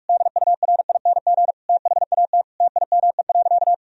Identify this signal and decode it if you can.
Morse code, heard as a keyed tone, about 41 wpm, 700 Hz. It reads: DURING THAT TIME.
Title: Sentences from Top 1000 Words - Review Part 2 - 45wpm